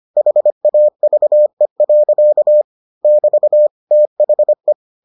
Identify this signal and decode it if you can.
Morse code, heard as a keyed tone, about 25 wpm, 600 Hz. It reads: HAVE. <BT> THE